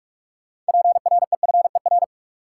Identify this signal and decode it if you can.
Morse code, heard as a keyed tone, about 46 wpm, 700 Hz. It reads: PREFER
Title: Sentences from Top 1100 Words - New Word Drill Alt - 45wpm